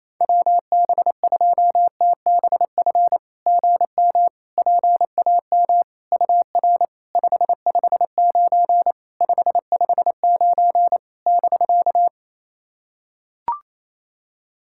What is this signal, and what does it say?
Morse code, heard as a keyed tone, about 28 wpm, 700 Hz. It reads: WB2TBF GM PAM UR 559 559 <BK>
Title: SOTA QSO Practice 28wpm